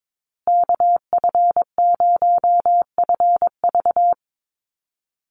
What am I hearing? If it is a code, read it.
Morse code, heard as a keyed tone, about 22 wpm, 700 Hz. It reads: KF0FV
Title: USA Call Signs Repeated 3x 22wpm